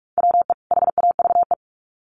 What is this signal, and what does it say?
Morse code, heard as a keyed tone, about 46 wpm, 700 Hz. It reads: WE HAVE
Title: Sets of 2 Words 45wpm